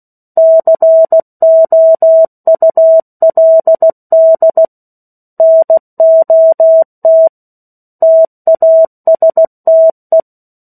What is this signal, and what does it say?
Morse code, heard as a keyed tone, about 16 wpm, 650 Hz. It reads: COULD NOT TASTE
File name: 03-Chapter-03 16 WPM